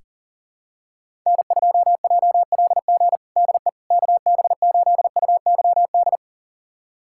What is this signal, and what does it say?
Morse code, heard as a keyed tone, about 40 wpm, 700 Hz. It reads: N2JLG DE KB8UYD
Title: USA Callsign DE Callsign 40wpm